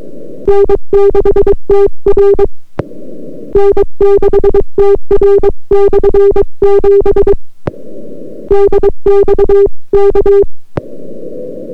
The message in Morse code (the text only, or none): N6TR N6TR/7 DX K